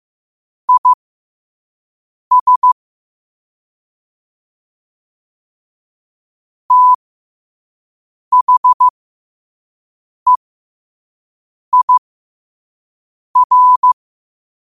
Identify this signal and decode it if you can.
Morse code, heard as a keyed tone, about 15 wpm, spaced as for 5 wpm, 1000 Hz. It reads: IS THEIR